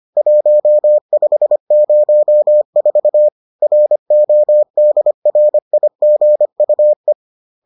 Morse code, heard as a keyed tone, about 25 wpm, 600 Hz. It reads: 1504 RODRIGUE
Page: bulletin UFT sous forme texte et en CW